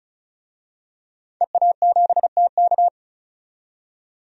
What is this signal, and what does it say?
Morse code, heard as a keyed tone, about 36 wpm, 700 Hz. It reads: EA7TK
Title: DXCC-100 Call Signs - 5 Characters - 35wpm